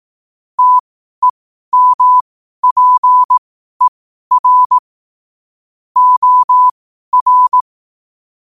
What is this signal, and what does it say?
Morse code, heard as a keyed tone, about 18 wpm, spaced as for 12 wpm, 1000 Hz. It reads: TEMPER OR
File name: qotdfiles / 12wpm